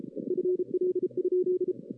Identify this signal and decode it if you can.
Morse code, heard as a keyed tone, about 33 wpm, 380 Hz. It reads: FL?
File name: 2016 arrl dx cw / 2016-ARRL-DX-CW-K5TR- - 0-5